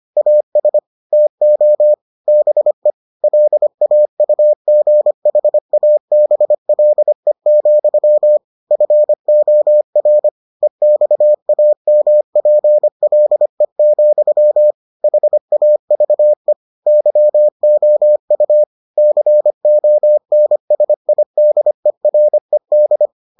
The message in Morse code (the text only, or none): AS TO BE LAUGHABLE, FOR EXAMPLE, HAVE YOU CONSIDERED